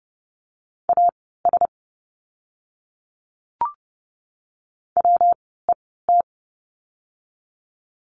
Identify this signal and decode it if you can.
Morse code, heard as a keyed tone, about 30 wpm, spaced as for 17 wpm, 700 Hz. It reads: AS WET
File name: course / 30:10wpm / 115 Lesson 19f - Letter w - Words 30wpm